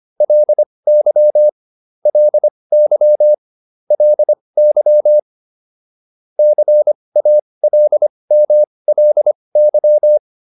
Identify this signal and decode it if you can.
Morse code, heard as a keyed tone, about 25 wpm, spaced as for 20 wpm, 600 Hz. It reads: LY LY LY CALMLY
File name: LY6 20